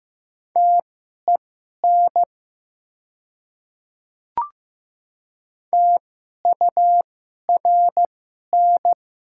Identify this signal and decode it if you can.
Morse code, heard as a keyed tone, about 15 wpm, spaced as for 11 wpm, 700 Hz. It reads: TEN TURN
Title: 2x Character Spacing - Top 500 Words - 15wpm